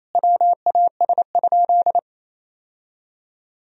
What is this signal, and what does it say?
Morse code, heard as a keyed tone, about 28 wpm, 700 Hz. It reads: WAS?
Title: Sentences from Top 1000 Words - Review Part 3 - 28wpm